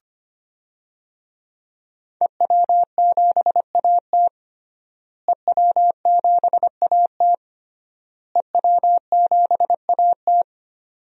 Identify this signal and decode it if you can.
Morse code, heard as a keyed tone, about 25 wpm, 700 Hz. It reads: EW7AT EW7AT EW7AT
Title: DXCC-100 Call Signs - 5 Characters - Repeated 3x - 25wpm